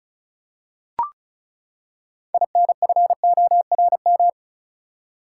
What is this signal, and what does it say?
Morse code, heard as a keyed tone, about 34 wpm, 700 Hz. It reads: INFORM
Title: Six Letter Words - 35wpm